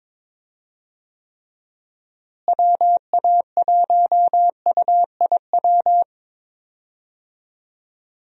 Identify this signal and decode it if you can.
Morse code, heard as a keyed tone, about 22 wpm, 700 Hz. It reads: WA1UIW
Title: USA Call Signs - 6 Characters - Repeated 3x - 22wpm